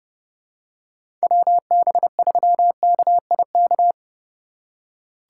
Morse code, heard as a keyed tone, about 30 wpm, 690 Hz. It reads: WB3KIK
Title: USA Call Signs - 4x Character Spacing - 30wpm